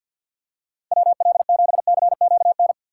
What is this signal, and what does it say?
Morse code, heard as a keyed tone, about 49 wpm, 700 Hz. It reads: WL7CQN
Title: USA Call Signs - 6 Characters - 3x Character Spacing - 50wpm